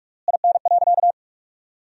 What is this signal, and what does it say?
Morse code, heard as a keyed tone, about 43 wpm, 700 Hz. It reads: IN.
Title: 3x Word-Spacing - Sentences from Top 400 Words - 45wpm